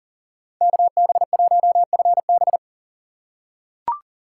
Morse code, heard as a keyed tone, about 40 wpm, 700 Hz. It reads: KB1FB
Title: USA Call Signs - 8x Character Spacing - 40wpm